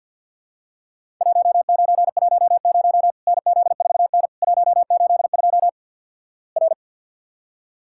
Morse code, heard as a keyed tone, about 52 wpm, 700 Hz. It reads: 1910 NZ4N 182 R
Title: Worked All Europe DX Contest - QTC Exchange with Repeats - 50wpm